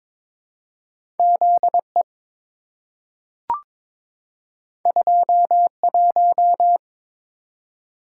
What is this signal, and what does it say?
Morse code, heard as a keyed tone, about 22 wpm, 700 Hz. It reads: ZE 21
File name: practice / 22wpm / Two Letter-Number Combinations - 22wpm